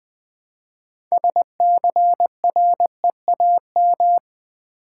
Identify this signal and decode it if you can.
Morse code, heard as a keyed tone, about 20 wpm, 700 Hz. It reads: SCREAM
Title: Build A Word - 3 Letters At A Time - 6-Letter Words - 20wpm